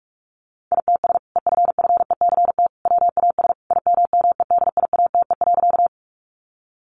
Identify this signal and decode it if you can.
Morse code, heard as a keyed tone, about 44 wpm, 700 Hz. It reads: ITS EFFECT WAS IMMEDIATE.
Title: Sentences from Top 1200 Words 45wpm